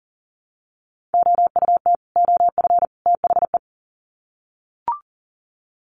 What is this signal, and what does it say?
Morse code, heard as a keyed tone, about 40 wpm, 700 Hz. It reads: OUT OF THE
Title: Sets of 3 Words 40wpm